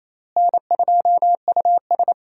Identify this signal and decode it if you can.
Morse code, heard as a keyed tone, about 28 wpm, 700 Hz. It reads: N2US
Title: USA Call Signs - 4 Characters - Repeated 3x - 28wpm